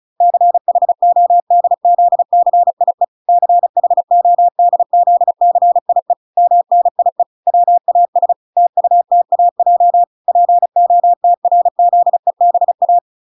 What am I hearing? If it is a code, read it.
Morse code, heard as a keyed tone, about 34 wpm, 700 Hz. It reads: CHODZCIE CHODZCIE MNIE WAS TUTAJ POTRZEBA